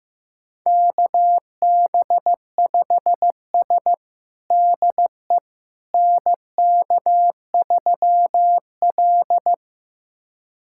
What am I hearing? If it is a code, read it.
Morse code, heard as a keyed tone, about 15 wpm, 700 Hz. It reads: KB5S DE NK3L
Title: USA Callsign DE Callsign - 4 Characters - 15wpm